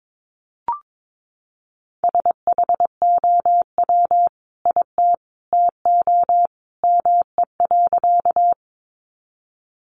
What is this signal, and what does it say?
Morse code, heard as a keyed tone, about 22 wpm, 700 Hz. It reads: SHOW IT TO ME.
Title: Sentences from Top 200 Words 22wpm